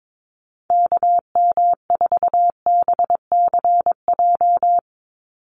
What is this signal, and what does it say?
Morse code, heard as a keyed tone, about 22 wpm, 700 Hz. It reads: KM4BCJ
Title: USA Call Signs 22wpm